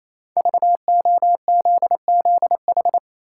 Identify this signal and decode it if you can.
Morse code, heard as a keyed tone, about 28 wpm, 700 Hz. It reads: VOZZH
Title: Code Groups - 5 Letter:Numbers - 28wpm